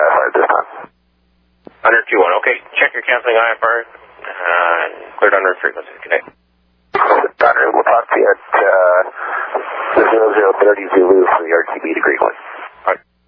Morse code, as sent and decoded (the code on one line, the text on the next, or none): none